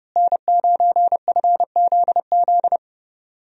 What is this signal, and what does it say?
Morse code, heard as a keyed tone, about 30 wpm, 700 Hz. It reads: N9FZZ
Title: USA Call Signs - 5 Characters - Repeated 3x - 30wpm